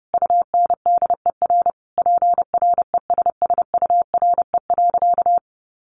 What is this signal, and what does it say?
Morse code, heard as a keyed tone, about 30 wpm, 700 Hz. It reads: UNDER PRESSURE.
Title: Sentences from Top 1400 Words 30wpm